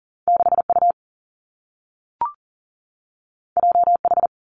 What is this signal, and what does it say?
Morse code, heard as a keyed tone, about 42 wpm, 700 Hz. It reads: 6 U J H
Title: Two Challenging Letter-Number Combinations 40wpm